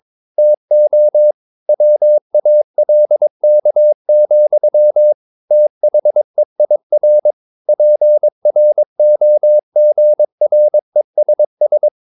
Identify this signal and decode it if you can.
Morse code, heard as a keyed tone, about 22 wpm, 600 Hz. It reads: TO WALK, THEIR PROGRESS
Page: Romanen Never af Ken Follet på engelsk.